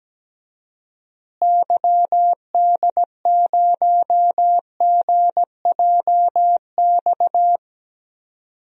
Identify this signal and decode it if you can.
Morse code, heard as a keyed tone, about 17 wpm, 700 Hz. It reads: YD0GJX
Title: DXCC-100 Call Signs - 6 Characters - Repeated 2x - 17wpm